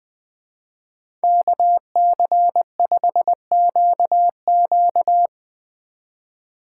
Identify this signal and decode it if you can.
Morse code, heard as a keyed tone, about 23 wpm, 700 Hz. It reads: KC5QQ
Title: Speed Racing - USA 2x2 Call Signs 20wpm